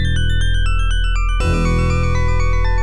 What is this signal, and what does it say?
There is no code.